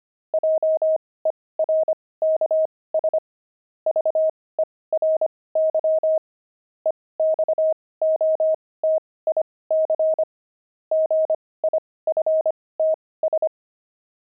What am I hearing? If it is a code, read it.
Morse code, heard as a keyed tone, about 25 wpm, spaced as for 18 wpm, 630 Hz. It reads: JERKS VERY EXOTIC GIFTS
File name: pangram-fs-18-cs-25-jerks